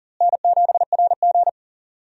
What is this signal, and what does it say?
Morse code, heard as a keyed tone, about 40 wpm, 700 Hz. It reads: N7RG
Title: USA Call Signs - 4 Characters - Repeated 2x - 40wpm